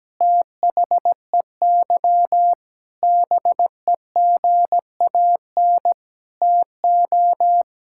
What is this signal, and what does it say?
Morse code, heard as a keyed tone, about 17 wpm, 700 Hz. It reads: THEY BEGAN TO
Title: Sentences from Top 1000 Words - Review Part 6 - 17wpm